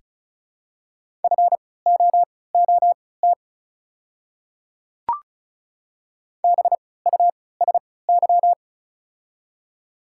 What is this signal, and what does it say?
Morse code, heard as a keyed tone, about 34 wpm, spaced as for 20 wpm, 690 Hz. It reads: FOOT BUSY
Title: 3x Character Spacing - Top 500 Words - 35wpm